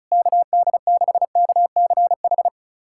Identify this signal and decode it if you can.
Morse code, heard as a keyed tone, about 36 wpm, 700 Hz. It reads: KD6KCH
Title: USA Callsign DE Callsign - 6 Characters - 35wpm